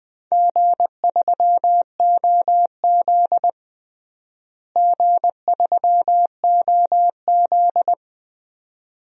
Morse code, heard as a keyed tone, about 21 wpm, 700 Hz. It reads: G3OZ G3OZ
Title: DXCC-100 Call Signs - 4 Characters - Repeated 2x - 20wpm